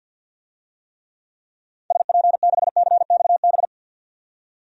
Morse code, heard as a keyed tone, about 52 wpm, 700 Hz. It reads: SP6CXB
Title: DXCC-100 Call Signs - 6 Characters - 50wpm